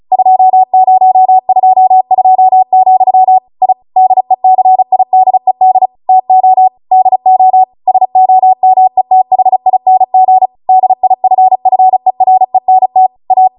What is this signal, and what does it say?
Morse code, heard as a keyed tone, about 36 wpm, 750 Hz. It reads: 2022, I DECIDED TO DO SOMETHING DIFFERENT A